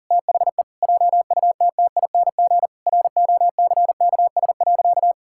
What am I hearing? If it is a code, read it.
Morse code, heard as a keyed tone, about 40 wpm, 700 Hz. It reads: THE JUTTING ROCKS.
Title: Pangram Challenge v2 - 40wpm